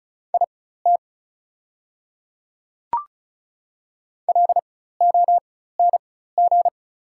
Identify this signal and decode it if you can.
Morse code, heard as a keyed tone, about 36 wpm, spaced as for 16 wpm, 700 Hz. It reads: IT LONG
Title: 4x Character Spacing - Top 100 Words - 35wpm